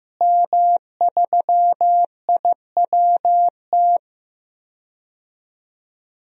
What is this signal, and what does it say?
Morse code, heard as a keyed tone, about 15 wpm, 700 Hz. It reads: M3IWT
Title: DXCC-100 Call Signs - 5 Characters - Repeated 3x - 15wpm